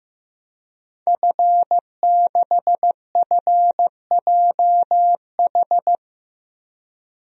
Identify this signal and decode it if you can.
Morse code, heard as a keyed tone, about 15 wpm, 700 Hz. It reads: F6FJH
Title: DXCC-100 Call Signs - 5 Characters - 15wpm